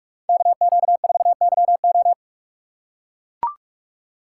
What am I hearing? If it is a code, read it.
Morse code, heard as a keyed tone, about 40 wpm, 700 Hz. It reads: KQ4YO